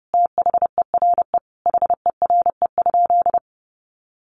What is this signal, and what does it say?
Morse code, heard as a keyed tone, about 28 wpm, 700 Hz. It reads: THERE HERE?
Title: Sentences from Top 1500 Words 30wpm